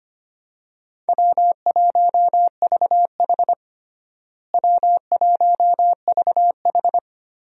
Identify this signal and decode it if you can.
Morse code, heard as a keyed tone, about 25 wpm, 700 Hz. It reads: W1VH W1VH
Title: USA Call Signs - 4 Characters - Repeated 3x - 25wpm